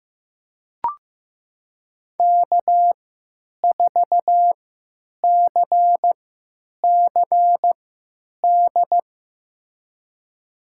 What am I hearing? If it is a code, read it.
Morse code, heard as a keyed tone, about 15 wpm, spaced as for 8 wpm, 700 Hz. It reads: K4CCD
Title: USA Call Signs - 5 Characters - 3x Character Spacing - 15wpm